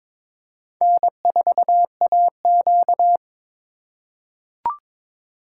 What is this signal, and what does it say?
Morse code, heard as a keyed tone, about 22 wpm, 700 Hz. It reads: N4AQ
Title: USA Call Signs - 4 Characters - 3x Character Spacing - 22wpm